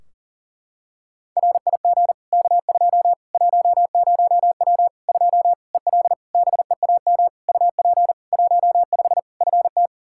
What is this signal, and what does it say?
Morse code, heard as a keyed tone, about 40 wpm, 700 Hz. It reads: RIG K2 10W 2 EL BEAM UP 15 FT
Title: Practice QSO - Exchange 2 - Info - 40wpm